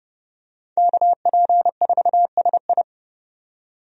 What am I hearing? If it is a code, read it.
Morse code, heard as a keyed tone, about 30 wpm, 700 Hz. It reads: KP4SI
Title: USA Call Signs - 5 Characters - 30wpm